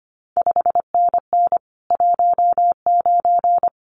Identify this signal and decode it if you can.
Morse code, heard as a keyed tone, about 25 wpm, 700 Hz. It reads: 5NN 19
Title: CQ WW DX Contest Exchange Encore - 25wpm